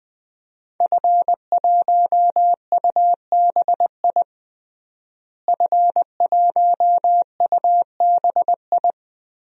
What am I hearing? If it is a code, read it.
Morse code, heard as a keyed tone, about 20 wpm, 700 Hz. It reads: F1UBI F1UBI
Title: DXCC-100 Call Signs - 5 Characters - Repeated 3x - 20wpm